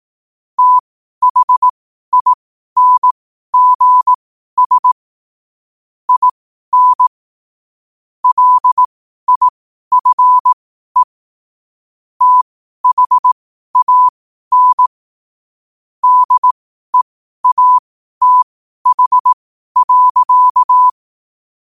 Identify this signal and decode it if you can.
Morse code, heard as a keyed tone, about 18 wpm, spaced as for 12 wpm, 1000 Hz. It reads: THINGS IN LIFE THAN DEATH.
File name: qotdfiles / 12wpm